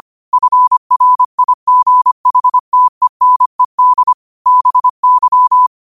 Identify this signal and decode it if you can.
Morse code, heard as a keyed tone, about 25 wpm, 1000 Hz. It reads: FRIGHTENED BY
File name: qotdfiles / 25wpm